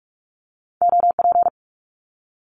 Morse code, heard as a keyed tone, about 46 wpm, 700 Hz. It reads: OP
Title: Abbreviations 45wpm